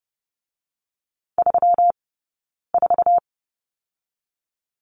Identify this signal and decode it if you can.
Morse code, heard as a keyed tone, about 30 wpm, 700 Hz. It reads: 3 4
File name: Lesson 20d - Number 3 - Two Characters 30wpm